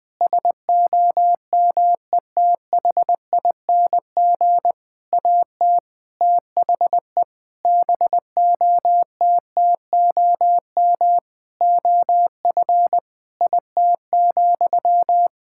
Morse code, heard as a keyed tone, about 21 wpm, 700 Hz. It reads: SOMETHING AT THE BOTTOM OF IT,
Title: Sentences from Top 1000 Words - Review Part 6 - 20wpm